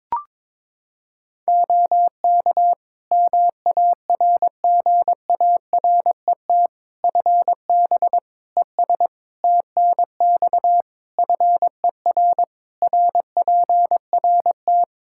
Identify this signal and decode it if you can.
Morse code, heard as a keyed tone, about 22 wpm, 700 Hz. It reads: OK MARGARET FB ES TNX FER RPRT
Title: Practice QSO - Exchange 2 - Info - 22wpm